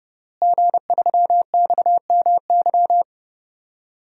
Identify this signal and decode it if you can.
Morse code, heard as a keyed tone, about 30 wpm, 700 Hz. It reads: G3XMY
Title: DXCC-100 Call Signs - 5 Characters - Repeated 2x - 30wpm